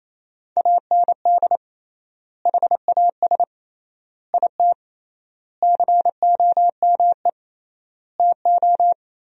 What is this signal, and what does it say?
Morse code, heard as a keyed tone, about 28 wpm, 700 Hz. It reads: AND HAS IT COME TO
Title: 3x Word-Spacing - Sentences from Top 100 Words - 28wpm